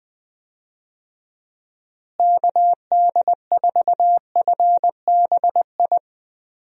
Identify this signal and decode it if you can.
Morse code, heard as a keyed tone, about 20 wpm, 700 Hz. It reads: KD4FBI
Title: USA Call Signs - 6 Characters - Repeated 3x - 20wpm